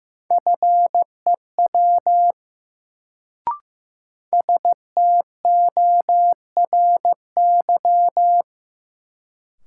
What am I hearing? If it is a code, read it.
Morse code, heard as a keyed tone, about 15 wpm, 700 Hz. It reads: FEW STORY
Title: Sentences from Top 300 Words - New Word Drill - 15wpm